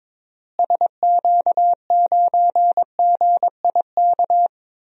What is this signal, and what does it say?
Morse code, heard as a keyed tone, about 22 wpm, 700 Hz. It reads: SQ9GIK
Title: DXCC-100 Call Signs - 6 Characters - Repeated 3x - 22wpm